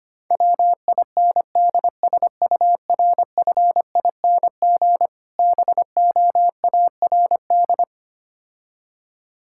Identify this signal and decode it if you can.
Morse code, heard as a keyed tone, about 25 wpm, 700 Hz. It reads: WINDSURFING BOARD
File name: practice / 25wpm / Difficult Words - Speed-Racing with Hints - 25wpm